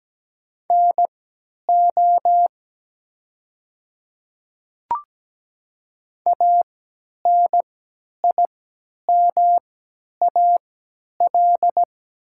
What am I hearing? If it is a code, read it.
Morse code, heard as a keyed tone, about 17 wpm, spaced as for 10 wpm, 700 Hz. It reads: NO ANIMAL
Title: 3x Character Spacing - Top 200 Words - 17wpm